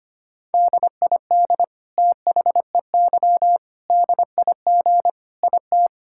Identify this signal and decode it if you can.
Morse code, heard as a keyed tone, about 25 wpm, 700 Hz. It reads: DID THEY DIG IT
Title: Sentences from Top 1600 Words 25wpm